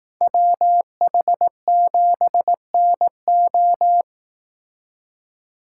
Morse code, heard as a keyed tone, about 18 wpm, 700 Hz. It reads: WH7NO
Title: Speed Racing - USA 2x2 Call Signs 15wpm